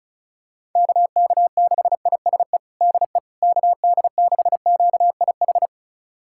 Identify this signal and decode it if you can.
Morse code, heard as a keyed tone, about 34 wpm, 700 Hz. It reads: KK6ISE DE KD6QIH
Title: USA Callsign DE Callsign - 6 Characters - 35wpm